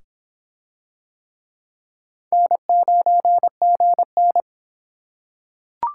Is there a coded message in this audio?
Yes: Morse code from 2.3 to 4.4 s, heard as a keyed tone, about 26 wpm, 700 Hz. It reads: N9GN